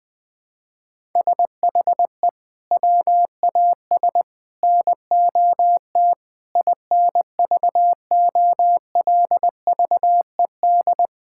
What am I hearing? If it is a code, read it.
Morse code, heard as a keyed tone, about 20 wpm, 700 Hz. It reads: SHE WAS NOT INVOLVED
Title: Sentences with 6-Word Phrases - Spoken Beginning - 20wpm